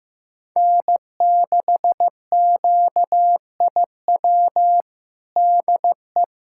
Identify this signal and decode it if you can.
Morse code, heard as a keyed tone, about 15 wpm, 700 Hz. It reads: N6QIW DE
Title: USA Callsign DE Callsign - 5 Characters - 15wpm